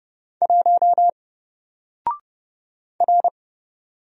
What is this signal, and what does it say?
Morse code, heard as a keyed tone, about 31 wpm, 700 Hz. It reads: 1 R
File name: Single-Letter-Number-30wpm